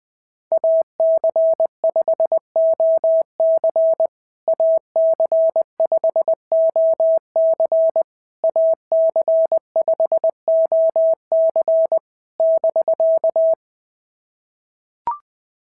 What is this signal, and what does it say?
Morse code, heard as a keyed tone, about 19 wpm, 650 Hz. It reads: AC5OC AC5OC AC5OC <BK>
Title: POTA QSO Practice - 20wpm